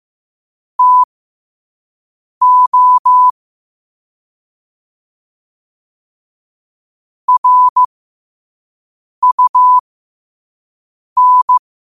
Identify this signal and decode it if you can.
Morse code, heard as a keyed tone, about 15 wpm, spaced as for 5 wpm, 1000 Hz. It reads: TO RUN
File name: qotdfiles / 5wpm